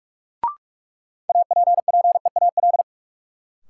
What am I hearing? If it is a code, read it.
Morse code, heard as a keyed tone, about 45 wpm, 700 Hz. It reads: APPEAL
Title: Sentences from Top 1300 Words - New Word Drill - 45wpm